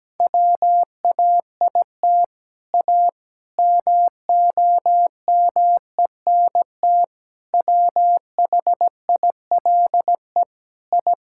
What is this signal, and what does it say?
Morse code, heard as a keyed tone, about 17 wpm, 700 Hz. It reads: WAIT A MOMENT WHILE I
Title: Sentences from Top 1000 Words - Review Part 5 - 17wpm